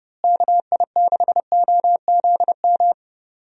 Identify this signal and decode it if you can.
Morse code, heard as a keyed tone, about 31 wpm, 700 Hz. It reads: KI6OZM
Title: USA Call Signs - 6 Characters - Repeated 3x - 30wpm